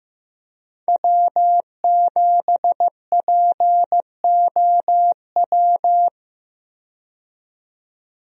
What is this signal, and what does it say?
Morse code, heard as a keyed tone, about 15 wpm, 700 Hz. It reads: W7POW
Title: USA Call Signs - 5 Characters - 15wpm